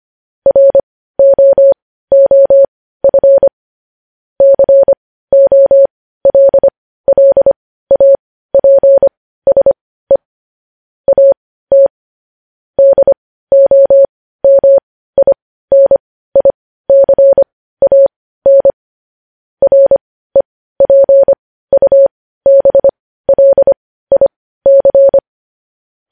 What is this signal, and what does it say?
Morse code, heard as a keyed tone, about 25 wpm, spaced as for 15 wpm, 550 Hz. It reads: ROOF COLLAPSE AT DOMINICAN REPUBLIC
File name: podcast / News headlines-1x-25x15